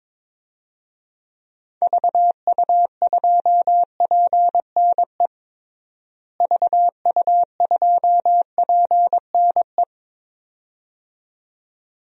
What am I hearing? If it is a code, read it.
Morse code, heard as a keyed tone, about 22 wpm, 700 Hz. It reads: VU2PNE VU2PNE
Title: DXCC-100 Call Signs - 6 Characters - Repeated 2x - 22wpm